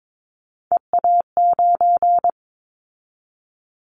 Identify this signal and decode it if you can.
Morse code, heard as a keyed tone, about 22 wpm, 700 Hz. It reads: EA9
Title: Callsign Prefixes for Top 100 DXCC - 22wpm